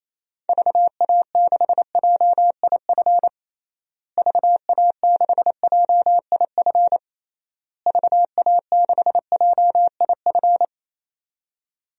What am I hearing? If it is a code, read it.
Morse code, heard as a keyed tone, about 28 wpm, 700 Hz. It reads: VA6JIF VA6JIF VA6JIF
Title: DXCC-100 Call Signs - 6 Characters - Repeated 3x - 28wpm